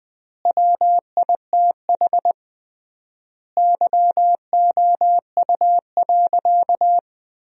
Morse code, heard as a keyed tone, about 21 wpm, 700 Hz. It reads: WITH YOU.
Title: 3x Word-Spacing - Sentences from Top 300 Words - 20wpm